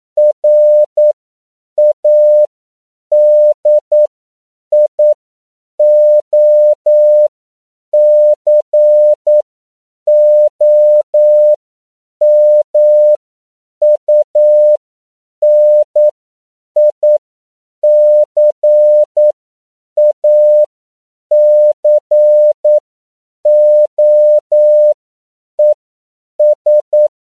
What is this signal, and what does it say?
Morse code, heard as a keyed tone, about 9 wpm, spaced as for 7 wpm, 600 Hz. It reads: RADIOCOMUNICACOES
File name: CW / 10PPM